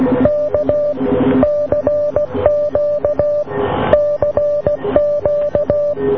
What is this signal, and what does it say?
Morse code, heard as a keyed tone, about 16 wpm, 590 Hz. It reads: K CQ CQ